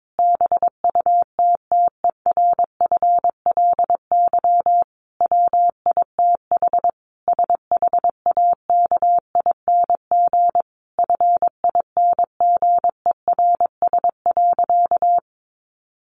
Morse code, heard as a keyed tone, about 22 wpm, 700 Hz. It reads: BUTTERFLY WITH SHAKING FINGERS.
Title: A Sound Of Thunder 22wpm